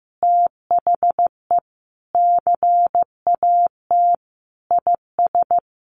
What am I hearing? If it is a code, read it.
Morse code, heard as a keyed tone, about 15 wpm, 700 Hz. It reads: THE CAT IS
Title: Easy Phrases 15wpm